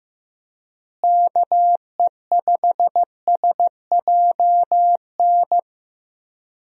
Morse code, heard as a keyed tone, about 15 wpm, 700 Hz. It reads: KE5SJN
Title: USA Call Signs - 6 Characters - Repeated 3x - 15wpm